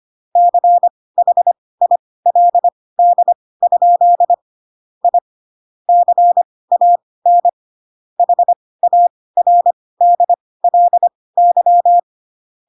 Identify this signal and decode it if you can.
Morse code, heard as a keyed tone, about 25 wpm, spaced as for 18 wpm, 700 Hz. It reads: CHILD? I CAN HARDLY
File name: Pinocchio / Pinocchio ch1 12 25 ik1hge